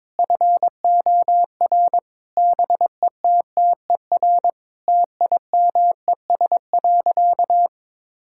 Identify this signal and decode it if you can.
Morse code, heard as a keyed tone, about 22 wpm, 700 Hz. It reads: FOR BETTER TIMES.
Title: Sentences from Top 1000 Words - Review Part 6 - 22wpm